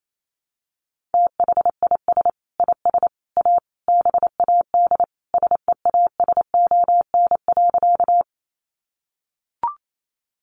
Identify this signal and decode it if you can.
Morse code, heard as a keyed tone, about 28 wpm, 700 Hz. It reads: THIS IS A BAD SEASON.
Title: Sentences from Top 1400 Words 28wpm